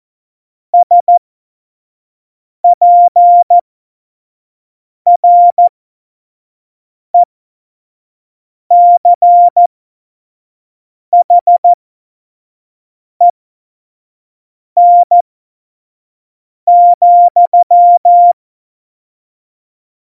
Morse code, heard as a keyed tone, about 14 wpm, spaced as for 5 wpm, 700 Hz. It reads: SPRECHEN,
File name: Morsekurs / Witze / Witz 05. September 17, Tempo 25